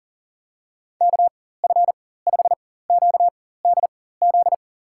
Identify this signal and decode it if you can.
Morse code, heard as a keyed tone, about 40 wpm, spaced as for 19 wpm, 700 Hz. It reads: KF5QDZ
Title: USA Call Signs - 6 Characters - 4x Character Spacing - 40wpm